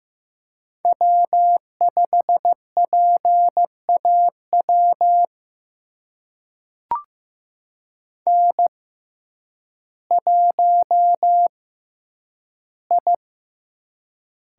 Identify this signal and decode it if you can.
Morse code, heard as a keyed tone, about 15 wpm, spaced as for 9 wpm, 700 Hz. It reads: W5PAW N1I
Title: USA Call Signs - 5 Characters - 6x Character Spacing - 15wpm